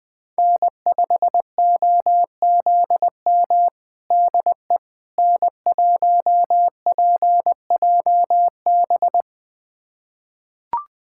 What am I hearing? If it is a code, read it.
Morse code, heard as a keyed tone, about 21 wpm, 700 Hz. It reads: N5OZM DE N1PJB
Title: USA Callsign DE Callsign - 5 Characters - 20wpm